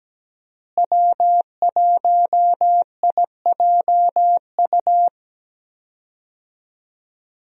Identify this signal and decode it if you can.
Morse code, heard as a keyed tone, about 17 wpm, 700 Hz. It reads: W1IJU
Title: USA Call Signs - 5 Characters - Repeated 3x - 17wpm